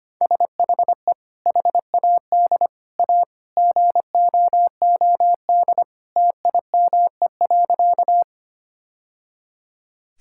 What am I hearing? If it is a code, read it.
Morse code, heard as a keyed tone, about 25 wpm, 690 Hz. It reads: SHE HAD A GOOD TIME.
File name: practice / 25wpm / Sentences with 2-Word Phrases - 25wpm